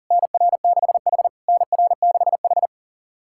Morse code, heard as a keyed tone, about 40 wpm, 690 Hz. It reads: NR6H NR6H
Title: SOTA QSO Practice 40wpm